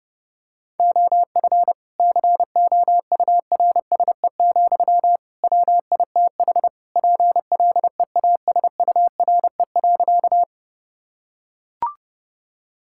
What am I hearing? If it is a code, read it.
Morse code, heard as a keyed tone, about 32 wpm, 700 Hz. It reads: OF COURSE, WITH PLEASURE.
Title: Sentences from Top 1600 Words 30wpm